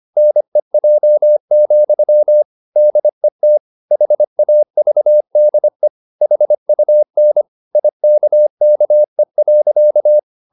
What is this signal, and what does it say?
Morse code, heard as a keyed tone, about 25 wpm, 600 Hz. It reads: NEJ, DET HAVDE HUN IKKE.
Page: Ferie 25 wpm